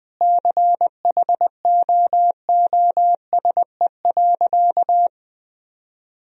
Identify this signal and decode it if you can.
Morse code, heard as a keyed tone, about 21 wpm, 700 Hz. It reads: CHOOSE.
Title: Sentences from Top 1000 Words - Review Part 4 - 20wpm